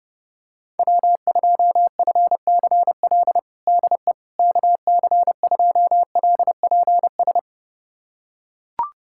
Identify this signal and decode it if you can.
Morse code, heard as a keyed tone, about 30 wpm, 700 Hz. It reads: W2FCL DE KC2LPS
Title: USA Callsign DE Callsign 30wpm